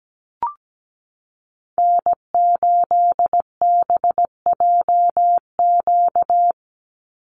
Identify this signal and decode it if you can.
Morse code, heard as a keyed tone, about 17 wpm, 700 Hz. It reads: N8BJQ
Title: Call Signs Repeated 3x - 17wpm